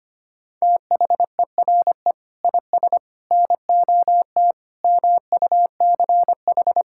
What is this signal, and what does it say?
Morse code, heard as a keyed tone, about 25 wpm, 700 Hz. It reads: THERE IS NOT MUCH
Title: Sentences from Top 1600 Words 25wpm